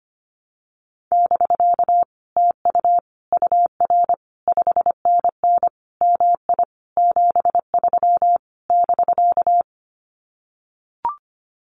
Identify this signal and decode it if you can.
Morse code, heard as a keyed tone, about 25 wpm, 700 Hz. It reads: <BK> TU UR 5NN MI 73 <BK>
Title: Common QSO Phrases Encore 25wpm